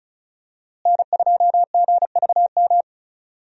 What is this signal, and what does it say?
Morse code, heard as a keyed tone, about 34 wpm, 700 Hz. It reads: N2GVM
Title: USA Call Signs - 5 Characters - Repeated 2x - 35wpm